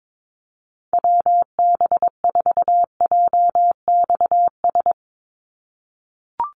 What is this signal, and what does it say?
Morse code, heard as a keyed tone, about 22 wpm, 700 Hz. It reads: WB4JXS
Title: USA Call Signs 22wpm